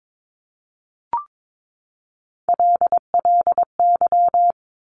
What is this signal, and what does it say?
Morse code, heard as a keyed tone, about 22 wpm, 700 Hz. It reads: LLY
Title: Top Three Letter Combinations in Words 22wpm